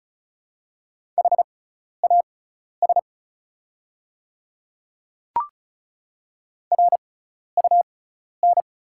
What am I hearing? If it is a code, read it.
Morse code, heard as a keyed tone, about 36 wpm, spaced as for 12 wpm, 690 Hz. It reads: HAS RUN
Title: 6x Character Spacing - Top 300 Words - 35wpm